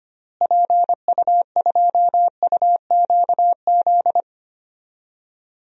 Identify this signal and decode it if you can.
Morse code, heard as a keyed tone, about 25 wpm, 700 Hz. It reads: PU2UQZ
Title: DXCC-100 Call Signs - 6 Characters - Repeated 3x - 25wpm